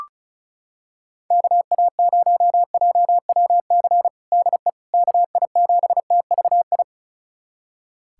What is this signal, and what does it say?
Morse code, heard as a keyed tone, about 36 wpm, 700 Hz. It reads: KA0JWC DE KI7TVI
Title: USA Callsign DE Callsign - 6 Characters - 35wpm